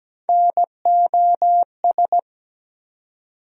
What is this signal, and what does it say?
Morse code, heard as a keyed tone, about 17 wpm, 700 Hz. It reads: NOS
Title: USA Call Sign Postfixes - 3 Letters 17wpm